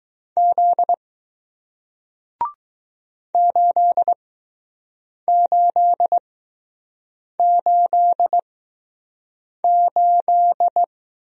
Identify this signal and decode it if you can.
Morse code, heard as a keyed tone, about 20 wpm, spaced as for 6 wpm, 700 Hz. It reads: Z 8888